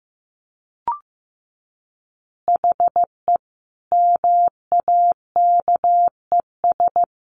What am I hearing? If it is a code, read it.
Morse code, heard as a keyed tone, about 15 wpm, 700 Hz. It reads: HE MAKES
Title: Additional Easy Phrases 15wpm